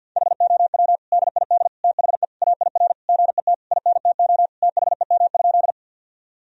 Morse code, heard as a keyed tone, about 53 wpm, 700 Hz. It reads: HOW DID THE AIR GET INTO THEM?
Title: Sentences from Top 1000 Words - Review Part 1 - 50wpm